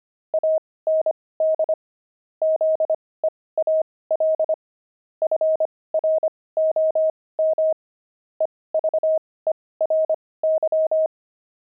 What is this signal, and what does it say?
Morse code, heard as a keyed tone, about 25 wpm, spaced as for 18 wpm, 630 Hz. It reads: AND ZEAL FROM EVERY
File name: pangram-fs-18-cs-25-zeal